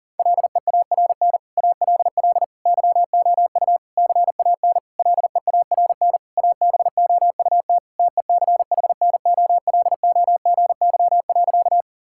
Morse code, heard as a keyed tone, about 40 wpm, 700 Hz. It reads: LEARN ALL YOU CAN LEARN ABOUT TECHNOLOGY.
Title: Sentences from Top 1000 Words - Review Part 5 - 40wpm